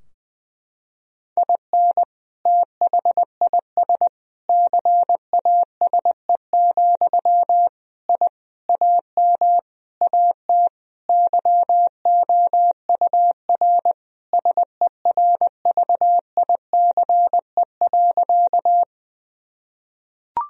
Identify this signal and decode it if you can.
Morse code, heard as a keyed tone, about 20 wpm, 700 Hz. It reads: IN THIS CASE, I AM AT YOUR SERVICE.
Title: Sentences from Top 1000 Words - Review Part 4 - 20wpm